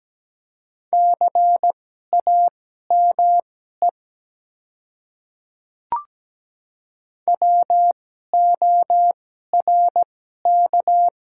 Morse code, heard as a keyed tone, about 17 wpm, spaced as for 12 wpm, 700 Hz. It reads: CAME WORK
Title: 2x Character Spacing - Top 300 Words - 17wpm